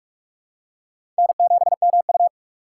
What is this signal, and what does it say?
Morse code, heard as a keyed tone, about 47 wpm, 700 Hz. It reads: N7MU
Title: USA Call Signs - 4x Character Spacing - 45wpm